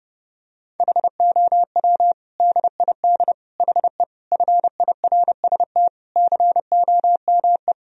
Morse code, heard as a keyed tone, about 30 wpm, 700 Hz. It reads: HOW DID HE FIRST COME
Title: Sentences from Top 1000 Words - Review Part 2 - 30wpm